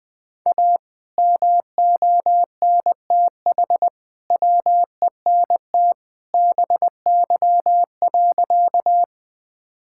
Morse code, heard as a keyed tone, about 19 wpm, 700 Hz. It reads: A MONTH WENT BY.
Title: Sentences from Top 1000 Words - Review Part 5 - 20wpm